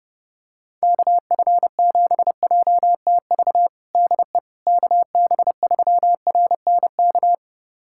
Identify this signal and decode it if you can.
Morse code, heard as a keyed tone, about 30 wpm, 700 Hz. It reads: KF7JTV DE KB3RNK
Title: USA Callsign DE Callsign 30wpm